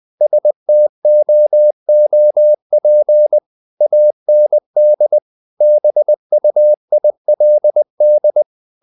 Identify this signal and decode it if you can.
Morse code, heard as a keyed tone, about 20 wpm, 600 Hz. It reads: STOOP AND BUILD
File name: CW / Practice / If20